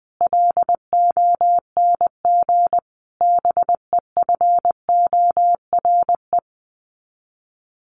Morse code, heard as a keyed tone, about 20 wpm, 700 Hz. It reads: LONG BEFORE
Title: Sets of 5 Words Encore2 20wpm